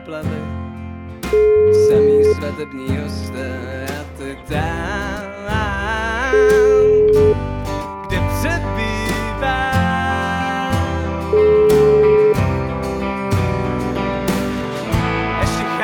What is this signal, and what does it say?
Ringback tone (Europe, 425 Hz), with speech.